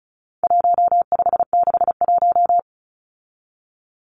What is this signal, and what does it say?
Morse code, heard as a keyed tone, about 34 wpm, 700 Hz. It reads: 1561
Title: Numbers - 4-Digit 35wpm